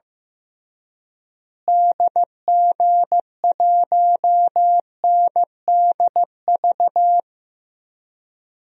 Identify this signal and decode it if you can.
Morse code, heard as a keyed tone, about 15 wpm, 700 Hz. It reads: DG1NDV
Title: DXCC-100 Call Signs - 6 Characters - Repeated 3x - 15wpm